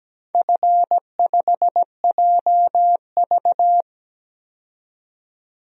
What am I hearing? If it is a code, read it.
Morse code, heard as a keyed tone, about 17 wpm, 700 Hz. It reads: F5JV
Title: DXCC-100 Call Signs - 4 Characters - 17wpm